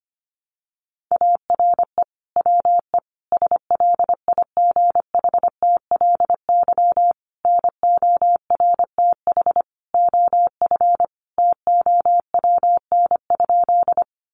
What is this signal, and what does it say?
Morse code, heard as a keyed tone, about 25 wpm, 700 Hz. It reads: ARE WE SLIGHTLY NORTH OF TOWN?
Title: Sentences from Top 1300 Words 25wpm